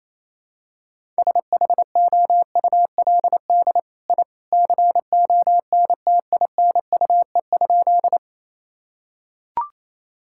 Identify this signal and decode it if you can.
Morse code, heard as a keyed tone, about 28 wpm, 700 Hz. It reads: SHOULD I CONTINUE?
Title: Sentences from Top 1000 Words - Review Part 6 - 28wpm